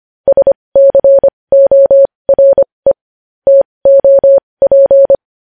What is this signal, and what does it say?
Morse code, heard as a keyed tone, about 25 wpm, spaced as for 20 wpm, 550 Hz. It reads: SCORE TOP
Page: news headlines in morse code at 20 wpm